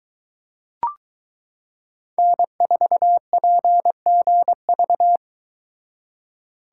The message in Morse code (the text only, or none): N4PGV